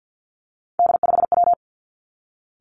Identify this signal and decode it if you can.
Morse code, heard as a keyed tone, about 50 wpm, 700 Hz. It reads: D5W
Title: Three Challenging Letter-Number Combinations 50wpm